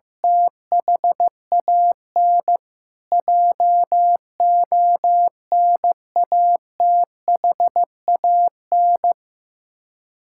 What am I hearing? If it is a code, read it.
Morse code, heard as a keyed tone, about 15 wpm, 700 Hz. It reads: THAN JONATHAN
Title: Practice QSO - Exchange 1 - Beginning - 15wpm